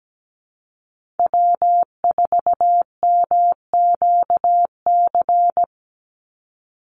Morse code, heard as a keyed tone, about 17 wpm, 700 Hz. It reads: W4MQC
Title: Call Signs - 17wpm